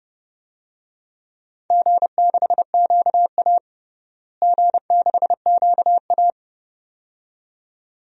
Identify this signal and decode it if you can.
Morse code, heard as a keyed tone, about 30 wpm, 700 Hz. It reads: G6QA G6QA
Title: DXCC-100 Call Signs - 4 Characters - Repeated 2x - 30wpm